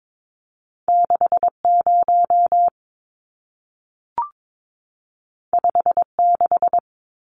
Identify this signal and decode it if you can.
Morse code, heard as a keyed tone, about 22 wpm, 700 Hz. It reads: 60 56
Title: Numbers - 2-Digit 22wpm